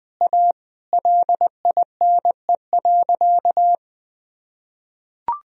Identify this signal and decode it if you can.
Morse code, heard as a keyed tone, about 20 wpm, 700 Hz. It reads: A LINE.
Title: Sentences from Top 1000 Words - Review Part 1 - 20wpm